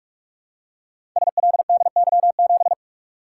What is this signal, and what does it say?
Morse code, heard as a keyed tone, about 44 wpm, 700 Hz. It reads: SLDY7
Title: Code Groups - 5 Letter:Numbers - 45wpm